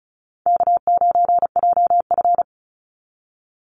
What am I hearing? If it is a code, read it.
Morse code, heard as a keyed tone, about 34 wpm, 700 Hz. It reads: K9JF
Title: Easy Call Signs Repeated 2x - 35wpm